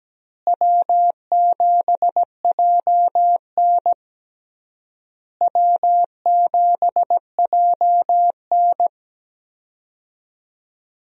Morse code, heard as a keyed tone, about 17 wpm, 700 Hz. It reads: W7JN W7JN
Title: USA Call Signs - 4 Characters - Repeated 3x - 17wpm